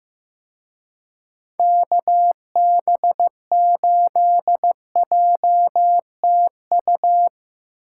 Morse code, heard as a keyed tone, about 17 wpm, 700 Hz. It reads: KB8JTU
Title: Speed Racing - USA 2x3 Call Signs 15wpm